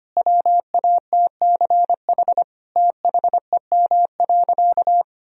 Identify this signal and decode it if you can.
Morse code, heard as a keyed tone, about 25 wpm, 700 Hz. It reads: WATCH THEM.
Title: Sentences from Top 1000 Words - Review Part 1 - 25wpm